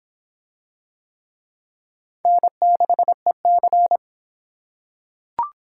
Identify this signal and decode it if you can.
Morse code, heard as a keyed tone, about 23 wpm, 700 Hz. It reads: N6EC